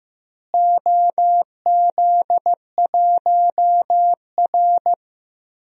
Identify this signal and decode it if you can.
Morse code, heard as a keyed tone, about 15 wpm, 700 Hz. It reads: OZ1R
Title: DXCC-100 Call Signs - 4 Characters - Repeated 3x - 15wpm